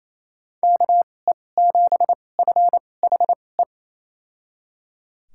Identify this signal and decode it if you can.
Morse code, heard as a keyed tone, about 28 wpm, spaced as for 20 wpm, 700 Hz. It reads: KE7FHE
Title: USA Call Signs - 6 Characters - 2x Character Spacing - 28wpm